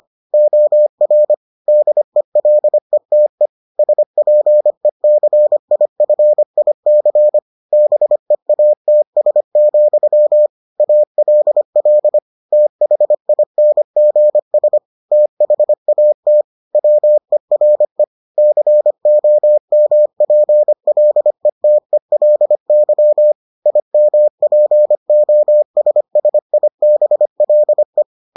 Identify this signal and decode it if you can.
Morse code, heard as a keyed tone, about 25 wpm, 600 Hz. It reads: OR DELETE SPECIFIC BEATS, ALL THINGS THAT WERE COMPLETELY IMPOSSIBLE